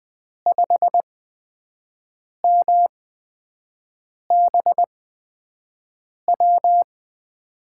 Morse code, heard as a keyed tone, about 20 wpm, spaced as for 5 wpm, 700 Hz. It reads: 5MBW
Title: USA Call Signs - 5 Characters - 8x Character Spacing - 20wpm